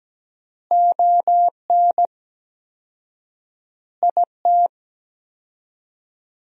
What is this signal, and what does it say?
Morse code, heard as a keyed tone, about 17 wpm, 700 Hz. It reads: ON IT
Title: 4x Word-Spacing - Sentences from Top 400 Words - 17wpm